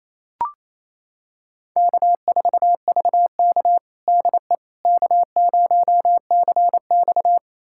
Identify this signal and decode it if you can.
Morse code, heard as a keyed tone, about 28 wpm, 700 Hz. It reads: K4VK DE K0CX
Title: USA Callsign DE Callsign - 4 Characters - 28wpm